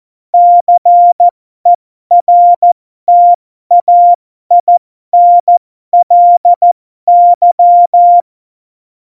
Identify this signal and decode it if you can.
Morse code, heard as a keyed tone, about 14 wpm, spaced as for 12 wpm, 700 Hz. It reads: CERTAINLY
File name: Morsekurs / Witze